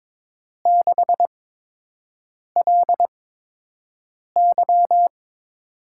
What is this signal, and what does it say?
Morse code, heard as a keyed tone, about 22 wpm, spaced as for 6 wpm, 700 Hz. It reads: 6LY
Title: USA Call Signs - 8x Character Spacing - 22wpm